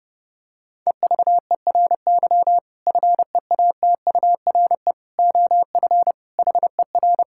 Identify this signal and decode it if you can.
Morse code, heard as a keyed tone, about 30 wpm, 700 Hz. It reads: EVERY FEATURE OF HER
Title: Sentences from Top 1000 Words - Review Part 5 - 30wpm